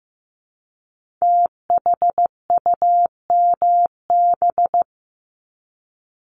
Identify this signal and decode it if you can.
Morse code, heard as a keyed tone, about 15 wpm, 700 Hz. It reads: THUMB
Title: Multi-letter Phonemes Repeated 3x 15wpm